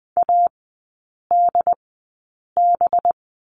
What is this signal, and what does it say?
Morse code, heard as a keyed tone, about 20 wpm, spaced as for 8 wpm, 700 Hz. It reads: ADB
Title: 184 Lesson 30e - Letter b - Three Characters 20wpm